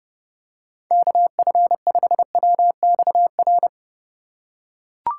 Morse code, heard as a keyed tone, about 30 wpm, 700 Hz. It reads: KF5WXR
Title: USA Call Signs - 3x Character Spacing - 30wpm